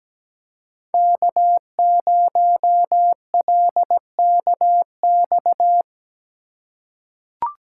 Morse code, heard as a keyed tone, about 17 wpm, 700 Hz. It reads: K0LKX
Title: USA Call Signs - 5 Characters - 4x Character Spacing - 17wpm